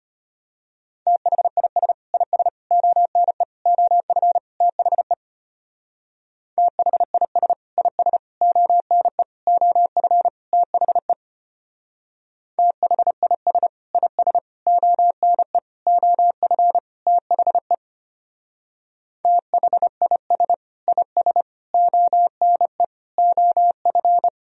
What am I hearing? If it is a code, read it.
Morse code, heard as a keyed tone, about 29 wpm, 700 Hz. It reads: THIS IS ONE OF THE THIS IS ONE OF THE THIS IS ONE OF THE THIS IS ONE OF
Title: Speed-Racing - Sets of 5 Words 25wpm